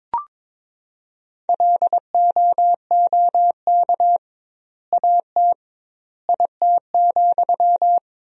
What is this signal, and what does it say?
Morse code, heard as a keyed tone, about 22 wpm, 700 Hz. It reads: LOOK AT IT,
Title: 2x Word-Spacing - Sentences from Top 200 Words - 22wpm